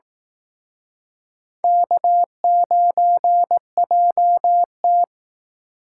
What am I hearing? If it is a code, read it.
Morse code, heard as a keyed tone, about 19 wpm, 700 Hz. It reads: K9JT